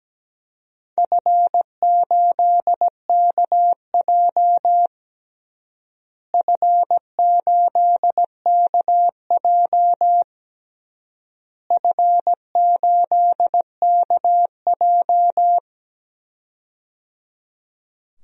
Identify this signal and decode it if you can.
Morse code, heard as a keyed tone, about 17 wpm, 700 Hz. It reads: F8KJ F8KJ F8KJ
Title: DXCC-100 Call Signs - 4 Characters - Repeated 3x - 17wpm